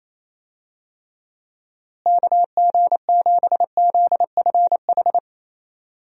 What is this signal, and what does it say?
Morse code, heard as a keyed tone, about 28 wpm, 700 Hz. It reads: KG7ZFH
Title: USA Call Signs - 6 Characters - Repeated 2x - 28wpm